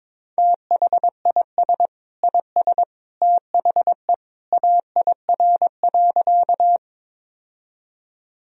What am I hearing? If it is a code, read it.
Morse code, heard as a keyed tone, about 22 wpm, 700 Hz. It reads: THIS IS THE AIR.
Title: Sentences from Top 1000 Words - Review Part 1 - 22wpm